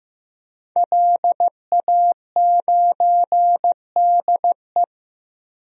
Morse code, heard as a keyed tone, about 15 wpm, 700 Hz. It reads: LA9DE
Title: DXCC-100 Call Signs - 5 Characters - Repeated 2x - 15wpm